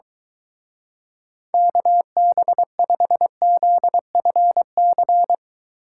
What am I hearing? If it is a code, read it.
Morse code, heard as a keyed tone, about 23 wpm, 700 Hz. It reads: KB5ZFC